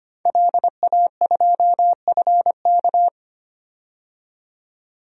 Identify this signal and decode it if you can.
Morse code, heard as a keyed tone, about 25 wpm, 700 Hz. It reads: LA2FK
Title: DXCC-100 Call Signs - 5 Characters - 25wpm